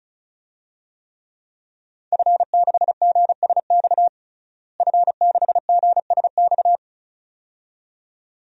Morse code, heard as a keyed tone, about 34 wpm, 700 Hz. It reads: F6GSX F6GSX
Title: DXCC-100 Call Signs - 5 Characters - Repeated 2x - 35wpm